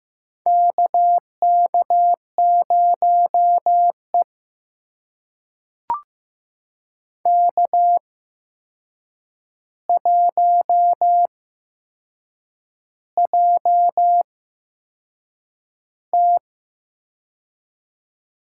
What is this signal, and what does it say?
Morse code, heard as a keyed tone, about 15 wpm, spaced as for 6 wpm, 700 Hz. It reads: KK0E K1JT
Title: USA Call Signs - 4 Characters - 8x Character Spacing - 15wpm